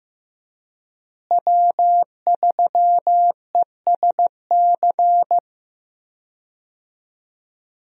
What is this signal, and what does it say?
Morse code, heard as a keyed tone, about 15 wpm, 700 Hz. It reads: W3ESC
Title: USA Call Signs - 5 Characters - Repeated 2x - 15wpm